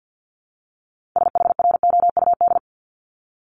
Morse code, heard as a keyed tone, about 50 wpm, 700 Hz. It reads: SHROUD
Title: Multi-letter Phonemes 50wpm